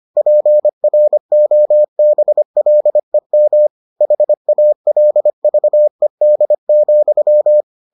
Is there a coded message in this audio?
Yes: Morse code from 0.2 to 7.6 s, heard as a keyed tone, about 25 wpm, 600 Hz. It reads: PROBLEM HALVED,